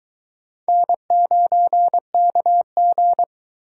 Morse code, heard as a keyed tone, about 21 wpm, 700 Hz. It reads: N9KG